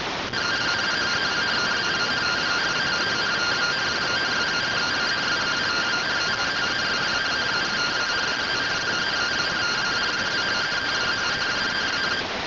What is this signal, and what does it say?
Two-tone teleprinter signal, about 170 Hz shift.
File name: findings / CT AXIAL -111.gen-22 rtty rx